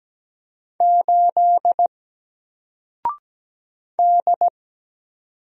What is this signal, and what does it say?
Morse code, heard as a keyed tone, about 17 wpm, 700 Hz. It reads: 8 D
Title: Single Character (Letter:Number:Punctuation:Prosign) Alt 17wpm